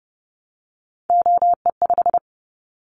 Morse code, heard as a keyed tone, about 30 wpm, 700 Hz. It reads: OE5
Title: Callsign Prefixes for Top 50 DXCC - 30wpm